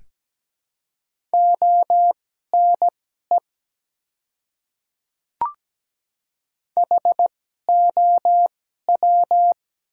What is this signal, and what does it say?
Morse code, heard as a keyed tone, about 17 wpm, spaced as for 12 wpm, 700 Hz. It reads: ONE HOW
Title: 2x Character Spacing - Top 200 Words - 17wpm